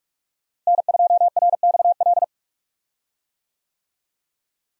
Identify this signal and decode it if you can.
Morse code, heard as a keyed tone, about 46 wpm, 700 Hz. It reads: N2RXL
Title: USA Call Signs - 5 Characters - Repeated 2x - 45wpm